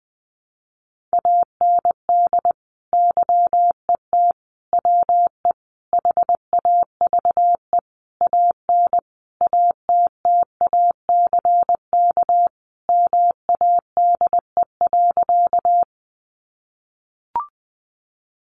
Morse code, heard as a keyed tone, about 20 wpm, 700 Hz. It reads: AND YET WE HAVE AN ATTACK MADE.
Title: Sentences from Top 1500 Words 20wpm